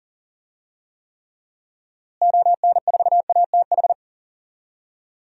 Morse code, heard as a keyed tone, about 40 wpm, 700 Hz. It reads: ON4ATH
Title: DXCC-100 Call Signs - 6 Characters - Repeated 2x - 40wpm